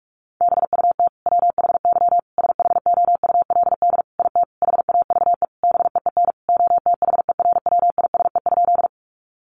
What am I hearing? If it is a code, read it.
Morse code, heard as a keyed tone, about 44 wpm, 700 Hz. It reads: BUT WHY SHOULD IT HAVE BEEN OTHERWISE?
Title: Sentences from Top 1200 Words 45wpm